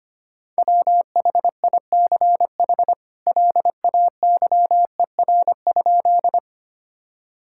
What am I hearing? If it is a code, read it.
Morse code, heard as a keyed tone, about 25 wpm, 700 Hz. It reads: WHICH LAYER?
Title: Sentences from Top 1600 Words 25wpm